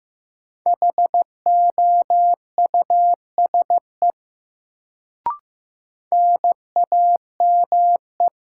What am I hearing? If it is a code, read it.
Morse code, heard as a keyed tone, about 15 wpm, 700 Hz. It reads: HOUSE NAME
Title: Sentences from Top 200 Words - New Word Drill - 15wpm